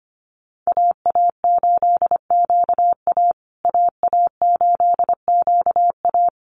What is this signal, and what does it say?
Morse code, heard as a keyed tone, about 25 wpm, 700 Hz. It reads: AA8QA AA8QA
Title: USA Call Signs Repeated 3x 25wpm